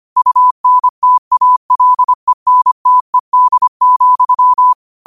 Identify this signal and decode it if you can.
Morse code, heard as a keyed tone, about 25 wpm, 1000 Hz. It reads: UNTALENTED,
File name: qotdfiles / 25wpm